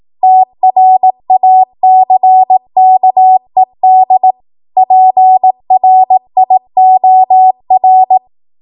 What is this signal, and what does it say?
Morse code, heard as a keyed tone, about 18 wpm, 750 Hz. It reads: TRACKED PRIOR
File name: Morse / 18WPM